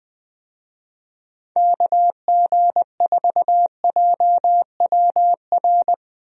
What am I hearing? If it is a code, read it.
Morse code, heard as a keyed tone, about 20 wpm, 700 Hz. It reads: KG4JWR
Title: USA Call Signs - 6 Characters - Repeated 2x - 20wpm